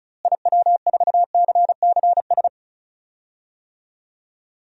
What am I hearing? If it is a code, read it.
Morse code, heard as a keyed tone, about 36 wpm, 700 Hz. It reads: IW4CCS
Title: DXCC-100 Call Signs - 6 Characters - Repeated 3x - 35wpm